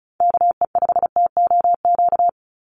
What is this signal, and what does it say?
Morse code, heard as a keyed tone, about 34 wpm, 700 Hz. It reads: KE5TOQ
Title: USA Call Signs 35wpm